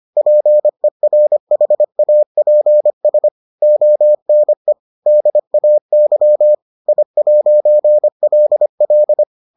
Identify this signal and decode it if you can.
Morse code, heard as a keyed tone, about 25 wpm, 600 Hz. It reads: PERHAPS ONE DAY I'LL